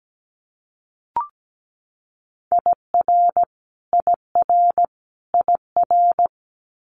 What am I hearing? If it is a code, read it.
Morse code, heard as a keyed tone, about 17 wpm, 700 Hz. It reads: IR IR IR
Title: Multi-letter Phonemes Repeated 3x 17wpm